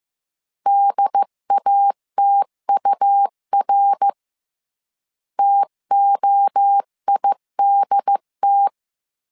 Morse code, heard as a keyed tone, about 15 wpm, 800 Hz. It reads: DATUR TOIDT
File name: CW / DET / 1 3 1500C S A 14wpm 2447